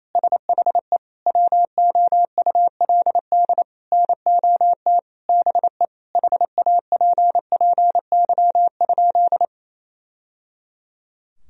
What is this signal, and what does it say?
Morse code, heard as a keyed tone, about 28 wpm, 700 Hz. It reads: SHE WOULD NOT BE HAPPY?
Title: Sentences from Top 1000 Words - Review Part 2 - 28wpm